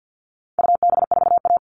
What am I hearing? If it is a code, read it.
Morse code, heard as a keyed tone, about 50 wpm, 700 Hz. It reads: UB4A
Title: All Cut Numbers - 4-Digit 50wpm